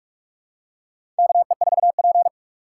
Morse code, heard as a keyed tone, about 44 wpm, 700 Hz. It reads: KE4P
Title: USA Call Signs - 4 Characters - 6x Character Spacing - 45wpm